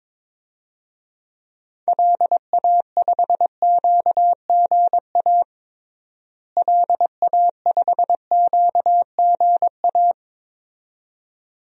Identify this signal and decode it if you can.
Morse code, heard as a keyed tone, about 22 wpm, 700 Hz. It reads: LA5QGA LA5QGA
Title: DXCC-100 Call Signs - 6 Characters - Repeated 2x - 22wpm